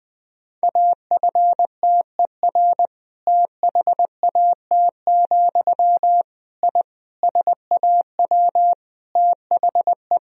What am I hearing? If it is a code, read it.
Morse code, heard as a keyed tone, about 20 wpm, 700 Hz. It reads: AFTER THAT, I SAW THE
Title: Sentences from Top 1000 Words - Review Part 1 - 20wpm